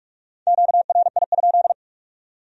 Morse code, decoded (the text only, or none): QRI?